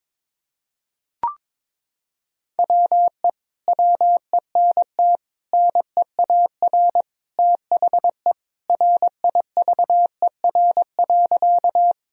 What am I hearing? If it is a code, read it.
Morse code, heard as a keyed tone, about 21 wpm, 700 Hz. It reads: WE WENT NEAR THE RIVER.
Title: Sentences from Top 1000 Words - Review Part 1 - 22wpm